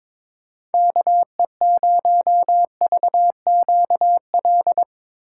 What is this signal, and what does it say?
Morse code, heard as a keyed tone, about 22 wpm, 700 Hz. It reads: KE0VQL
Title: USA Call Signs - 6 Characters - 2x Character Spacing - 22wpm